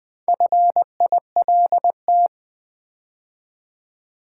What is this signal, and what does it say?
Morse code, heard as a keyed tone, about 20 wpm, 700 Hz. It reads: FILT
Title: Build A Word - 2 Letters At A Time - 6-Letter Words - 20wpm